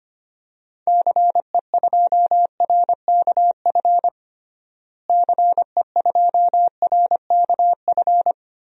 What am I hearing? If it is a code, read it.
Morse code, heard as a keyed tone, about 25 wpm, 700 Hz. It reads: CE2RKF CE2RKF
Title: DXCC-100 Call Signs - 6 Characters - Repeated 2x - 25wpm